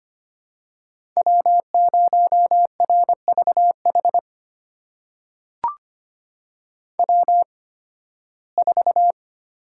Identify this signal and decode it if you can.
Morse code, heard as a keyed tone, about 25 wpm, spaced as for 16 wpm, 700 Hz. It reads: W0RVH W4
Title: USA Call Signs - 5 Characters - 8x Character Spacing - 25wpm